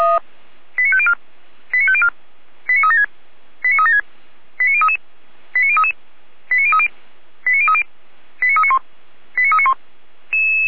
Five-tone selective call (ZVEI), heard as quick runs of five tones, tones of about 70 ms. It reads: 89504 89504 89376 89376 89033 89033 89030 89030 89391 89391
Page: /455/SDR/ZVEI/